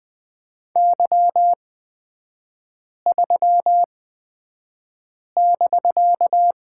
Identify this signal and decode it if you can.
Morse code, heard as a keyed tone, about 20 wpm, spaced as for 5 wpm, 700 Hz. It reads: Y3<BK>
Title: 250 Lesson 40e - BK - Three Characters 20wpm